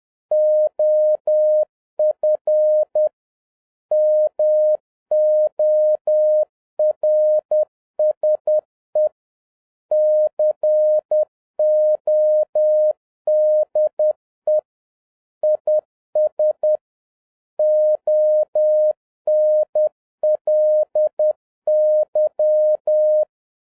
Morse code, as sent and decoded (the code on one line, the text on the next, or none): --- ..-. / -- --- .-. ... / -.-. --- -.. / .. ... / --- -. .-.. -.--
OF MORSE CODE IS ONLY